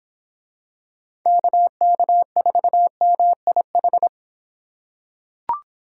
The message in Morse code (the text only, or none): KK4MIH